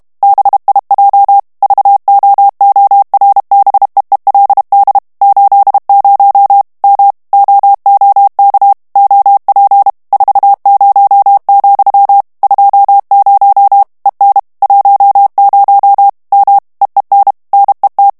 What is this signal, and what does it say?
Morse code, heard as a keyed tone, about 32 wpm, 800 Hz. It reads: BIJ VOORBEELD 80 M OOK OP 40, 20 EN 10 M EEN NET
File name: morse mp3 / HOLLANDAIS texte clair 1 vitesse 30 mots minute